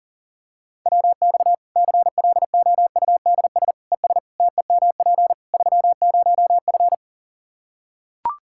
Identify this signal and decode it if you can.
Morse code, heard as a keyed tone, about 40 wpm, 700 Hz. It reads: WX CLOUDS ES TEMP 30F
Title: K1USN SST QSO Practice 40wpm